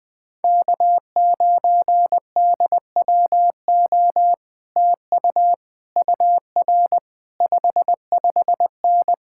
Morse code, heard as a keyed tone, about 20 wpm, 700 Hz. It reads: K9DWO TU UR 55N
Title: Speed-Racing Common QSO Phrases - 17wpm